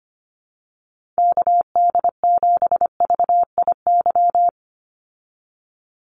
Morse code, heard as a keyed tone, about 25 wpm, 700 Hz. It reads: KD7VIY
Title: USA Call Signs 25wpm